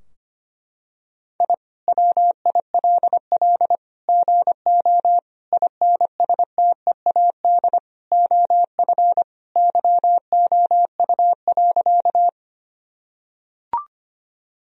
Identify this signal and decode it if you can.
Morse code, heard as a keyed tone, about 25 wpm, 700 Hz. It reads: I WILL GO INSTEAD OF YOU.
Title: Sentences from Top 1000 Words - Review Part 4 - 25wpm